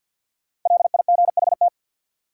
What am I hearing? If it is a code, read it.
Morse code, heard as a keyed tone, about 53 wpm, 700 Hz. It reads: LIGHT
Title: Sentences from Top 200 Words - New Word Drill - 50wpm